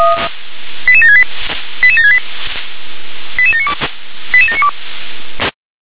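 Five-tone selective call (ZVEI), heard as quick runs of five tones, tones of about 70 ms. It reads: 88768 88768 88792 88792
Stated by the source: ZVEI five-tone selective call received on SDR.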